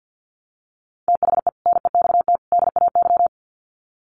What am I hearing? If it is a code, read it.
Morse code, heard as a keyed tone, about 48 wpm, 700 Hz. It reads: THE NEXT DAY
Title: Sets of 3 Words 50wpm